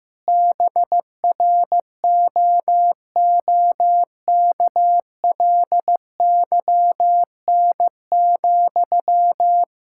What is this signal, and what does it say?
Morse code, heard as a keyed tone, about 15 wpm, 700 Hz. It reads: BROOKLYN,
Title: K1USN SST QSO Practice 15wpm